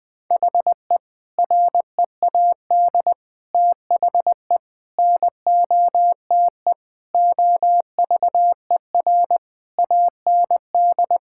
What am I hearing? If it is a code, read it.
Morse code, heard as a keyed tone, about 20 wpm, 700 Hz. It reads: HE READ THE NOTE OVER AND
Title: Sentences with 4-Word Phrases - 20wpm